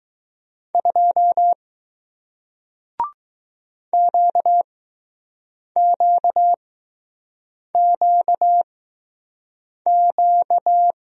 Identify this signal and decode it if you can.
Morse code, heard as a keyed tone, about 20 wpm, spaced as for 6 wpm, 700 Hz. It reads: 2 QQQQ